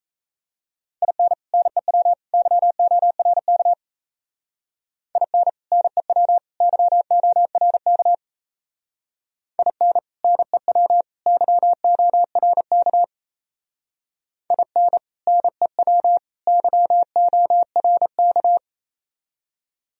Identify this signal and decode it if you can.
Morse code, heard as a keyed tone, about 36 wpm, 700 Hz. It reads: IN NEW YORK IN NEW YORK IN NEW YORK IN NEW YORK